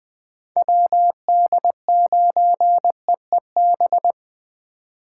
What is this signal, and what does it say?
Morse code, heard as a keyed tone, about 20 wpm, 700 Hz. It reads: WD9EEB
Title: USA Call Signs - 6 Characters - 4x Character Spacing - 20wpm